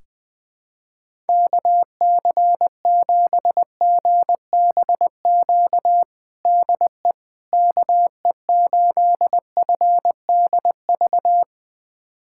Morse code, heard as a keyed tone, about 20 wpm, 700 Hz. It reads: KC7GBQ DE KE8FDV
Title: USA Callsign DE Callsign - 6 Characters - 20wpm